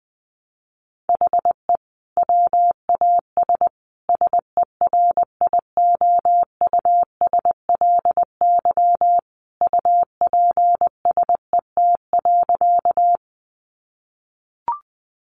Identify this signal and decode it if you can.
Morse code, heard as a keyed tone, about 20 wpm, 700 Hz. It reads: HE WAS SERIOUSLY UPSET.
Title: Sentences from Top 1200 Words 20wpm